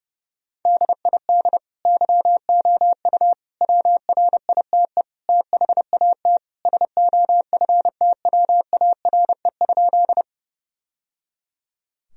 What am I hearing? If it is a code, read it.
Morse code, heard as a keyed tone, about 28 wpm, 700 Hz. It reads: DID YOU WRITE THAT SOFTWARE?
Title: Sentences from Top 1000 Words - Review Part 3 - 30wpm